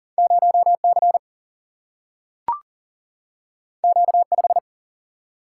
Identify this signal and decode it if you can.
Morse code, heard as a keyed tone, about 40 wpm, 700 Hz. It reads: 0C Q5
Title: Single-Letter Single-Number 40wpm